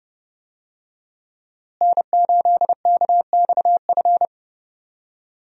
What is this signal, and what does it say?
Morse code, heard as a keyed tone, about 32 wpm, 700 Hz. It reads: N8KXF
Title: USA Call Signs - 5 Characters - Repeated 3x - 30wpm